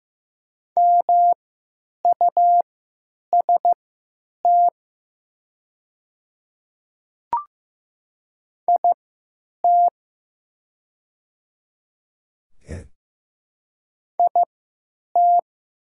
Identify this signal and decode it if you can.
Morse code, heard as a keyed tone, about 15 wpm, spaced as for 9 wpm, 700 Hz. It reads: MUST IT IT
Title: 3x Character Spacing - Top 200 Words - 15wpm